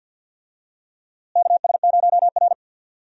Morse code, heard as a keyed tone, about 52 wpm, 700 Hz. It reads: KS0R
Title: USA Call Signs - 4 Characters - Repeated 2x - 50wpm